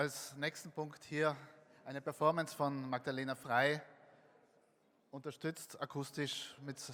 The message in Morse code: none

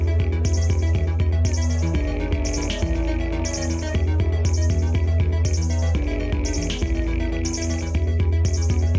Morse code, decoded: none

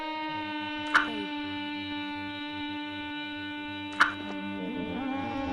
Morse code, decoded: none

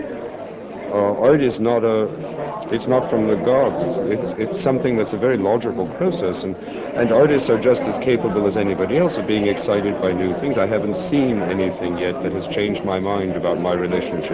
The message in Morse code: none

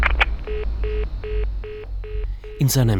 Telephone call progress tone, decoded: congestion tone

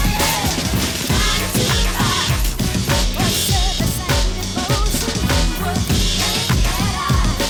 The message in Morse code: none

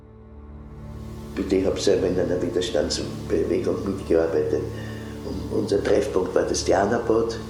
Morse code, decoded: none

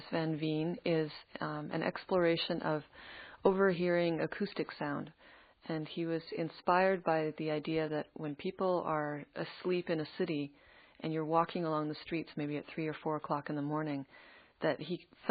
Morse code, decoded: none

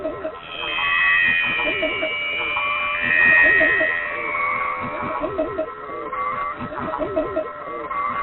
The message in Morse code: none

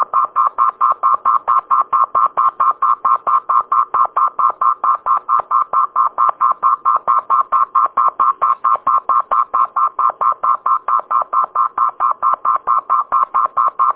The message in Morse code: none

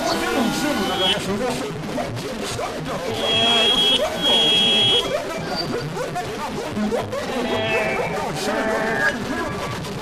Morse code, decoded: none

